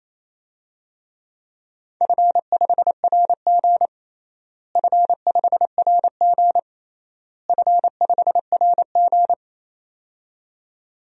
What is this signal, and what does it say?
Morse code, heard as a keyed tone, about 27 wpm, 700 Hz. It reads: F5RG F5RG F5RG
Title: DXCC-100 Call Signs - 4 Characters - Repeated 3x - 28wpm